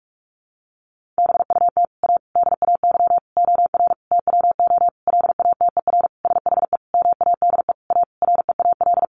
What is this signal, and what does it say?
Morse code, heard as a keyed tone, about 44 wpm, 700 Hz. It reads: BUT A DAY OR TWO LATER SHE MADE A REAL
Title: Sentences from Top 1100 Words 45wpm